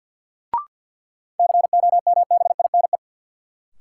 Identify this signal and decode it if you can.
Morse code, heard as a keyed tone, about 48 wpm, 700 Hz. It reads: COMBINE
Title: Sentences from Top 1600 Words - New Word Drill Alt - 50wpm